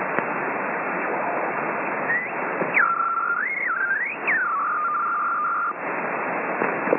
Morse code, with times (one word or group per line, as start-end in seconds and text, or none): none